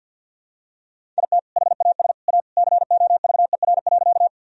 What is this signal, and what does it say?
Morse code, heard as a keyed tone, about 51 wpm, 700 Hz. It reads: IT HAS A COVER.
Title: Sentences from Top 1000 Words - Review Part 1 - 50wpm